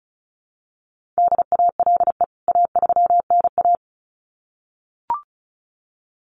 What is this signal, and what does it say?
Morse code, heard as a keyed tone, about 34 wpm, 700 Hz. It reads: DALE A3NA
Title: CWT Exchange 35wpm